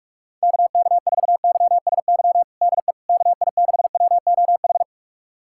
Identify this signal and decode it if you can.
Morse code, heard as a keyed tone, about 46 wpm, 700 Hz. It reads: KK4YSY DE KI6WOH
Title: USA Callsign DE Callsign 45wpm